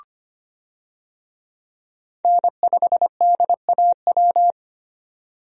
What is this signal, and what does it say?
Morse code, heard as a keyed tone, about 25 wpm, 700 Hz. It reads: N5DAW
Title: USA Call Signs - 5 Characters - 25wpm